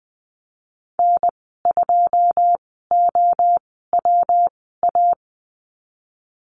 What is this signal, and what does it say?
Morse code, heard as a keyed tone, about 20 wpm, spaced as for 14 wpm, 700 Hz. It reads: N2OWA
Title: 154 Lesson 25g - Letter p - Call Signs 20wpm